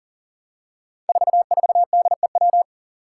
Morse code, heard as a keyed tone, about 40 wpm, 700 Hz. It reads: 44DEW
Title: Five Challenging Letter-Number Combinations 40wpm